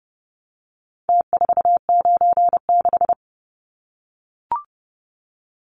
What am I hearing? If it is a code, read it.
Morse code, heard as a keyed tone, about 30 wpm, 700 Hz. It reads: T496
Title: Numbers - Contest Style - 4-Digit 30wpm